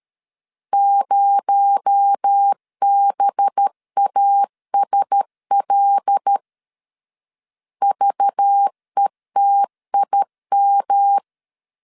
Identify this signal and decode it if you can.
Morse code, heard as a keyed tone, about 13 wpm, 800 Hz. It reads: 0BASL VETIM
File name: CW / DET / 1 6 1500C S B 12wpm 3431